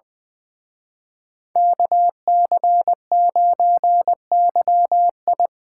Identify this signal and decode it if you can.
Morse code, heard as a keyed tone, about 24 wpm, 700 Hz. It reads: KC9YI
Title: Speed Racing - USA Call Signs 20wpm